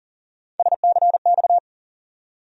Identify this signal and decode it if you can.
Morse code, heard as a keyed tone, about 40 wpm, 700 Hz. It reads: SCX
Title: USA Call Sign Postfixes - 3 Letters 40wpm